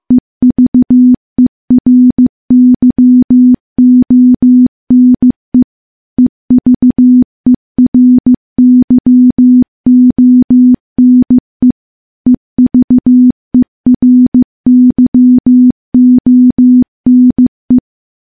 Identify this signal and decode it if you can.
Morse code, heard as a keyed tone, about 15 wpm, 250 Hz. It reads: EVERYONE EVERYONE EVERYONE
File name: morse-training-15WPM